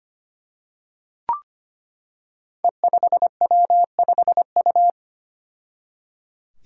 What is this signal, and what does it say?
Morse code, heard as a keyed tone, about 25 wpm, 700 Hz. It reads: E5W5U
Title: Five Challenging Letter-Number Combinations 25wpm